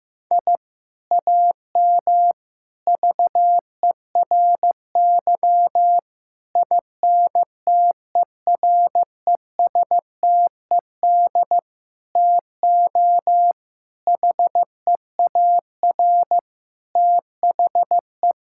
Morse code, heard as a keyed tone, about 16 wpm, 700 Hz. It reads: I AM VERY INTERESTED TO HEAR THE
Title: Sentences from Top 1000 Words - Review Part 6 - 15wpm